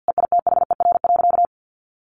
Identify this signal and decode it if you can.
Morse code, heard as a keyed tone, about 50 wpm, 700 Hz. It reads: EITHER.
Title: Sentences from Top 600 Words 50wpm